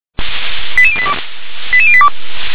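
Five-tone selective call (ZVEI), heard as quick runs of five tones, tones of about 70 ms. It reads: 88082 88082
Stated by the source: ZVEI five-tone call received on SDR.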